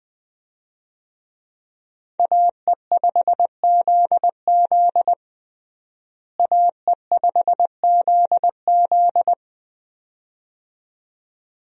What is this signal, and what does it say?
Morse code, heard as a keyed tone, about 21 wpm, 700 Hz. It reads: AE5ZZ AE5ZZ
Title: USA Call Signs - 5 Characters - Repeated 2x - 20wpm